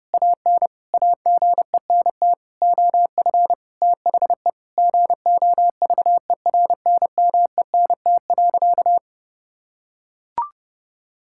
Morse code, heard as a keyed tone, about 30 wpm, 700 Hz. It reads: AN AGENT OF THE GOVERNMENT.
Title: Sentences from Top 1600 Words 30wpm